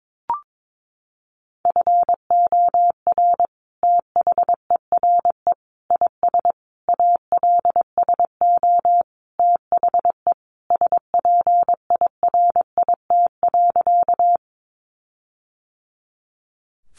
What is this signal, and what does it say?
Morse code, heard as a keyed tone, about 22 wpm, 700 Hz. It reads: FOR THERE IS ALSO THE SPIRIT.
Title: Sentences from Top 1400 Words 22wpm